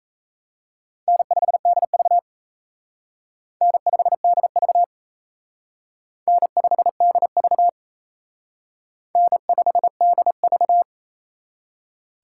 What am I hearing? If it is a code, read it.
Morse code, heard as a keyed tone, about 34 wpm, 700 Hz. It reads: N5DV N5DV N5DV N5DV
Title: Speed Racing - USA 1x2 Call Signs 28wpm